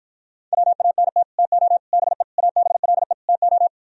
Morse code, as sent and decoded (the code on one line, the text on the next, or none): .-- .- -. - / - --- / -... / .- -... .-.. / - ---
WANT TO BE ABLE TO